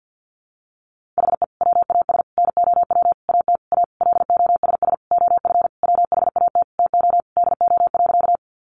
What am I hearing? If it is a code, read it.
Morse code, heard as a keyed tone, about 50 wpm, 700 Hz. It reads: HE WAS NOW AT A LOSS OF WHAT TO DO.
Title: Sentences from Top 1400 Words 50wpm